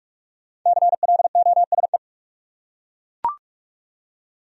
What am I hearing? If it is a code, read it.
Morse code, heard as a keyed tone, about 40 wpm, 700 Hz. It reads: CLOSE